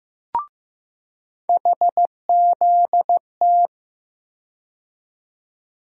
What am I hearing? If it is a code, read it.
Morse code, heard as a keyed tone, about 15 wpm, 700 Hz. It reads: HZT
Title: USA Call Sign Postfixes - 3 Letters 15wpm